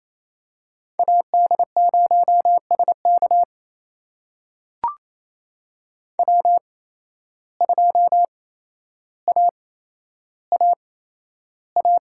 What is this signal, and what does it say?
Morse code, heard as a keyed tone, about 28 wpm, spaced as for 12 wpm, 700 Hz. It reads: AD0SK W2AAA
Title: USA Call Signs - 5 Characters - 8x Character Spacing - 28wpm